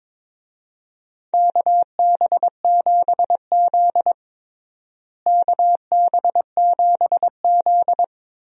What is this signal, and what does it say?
Morse code, heard as a keyed tone, about 22 wpm, 700 Hz. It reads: KB7Z KB7Z
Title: USA Call Signs - 4 Characters - Repeated 2x - 22wpm